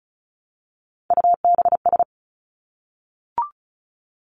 Morse code, heard as a keyed tone, about 36 wpm, 700 Hz. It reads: UBS
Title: Three Challenging Letter-Number Combinations 35wpm